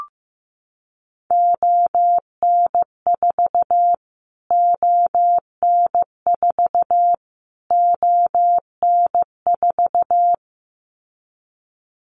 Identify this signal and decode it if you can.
Morse code, heard as a keyed tone, about 15 wpm, 700 Hz. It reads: ON4 ON4 ON4
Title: Callsign Prefixes for Top 25 DXCC - Repeated 3x - 15wpm